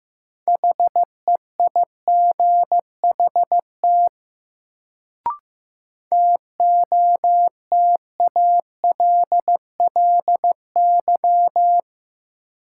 Morse code, heard as a keyed tone, about 15 wpm, 700 Hz. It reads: HEIGHT TOTALLY
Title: Sentences from Top 1600 Words - New Word Drill Alt - 15wpm